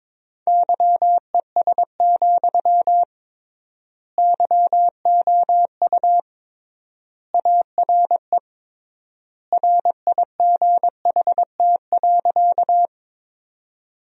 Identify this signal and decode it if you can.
Morse code, heard as a keyed tone, about 22 wpm, 700 Hz. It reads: YES, YOU ARE RIGHT.
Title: 3x Word-Spacing - Sentences from Top 500 Words - 22wpm